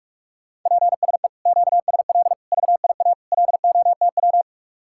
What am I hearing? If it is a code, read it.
Morse code, heard as a keyed tone, about 49 wpm, 700 Hz. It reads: PSE QSL VIA LOTW